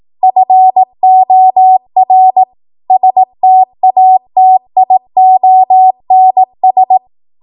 Morse code, heard as a keyed tone, about 18 wpm, 750 Hz. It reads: FOR STATIONS
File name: Morse / 18WPM